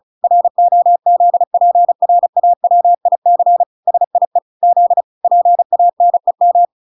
Morse code, heard as a keyed tone, about 34 wpm, 700 Hz. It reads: ROZPRAWIC SIE Z PANEM